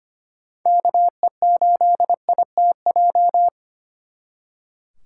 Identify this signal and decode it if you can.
Morse code, heard as a keyed tone, about 32 wpm, 700 Hz. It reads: KE8ITJ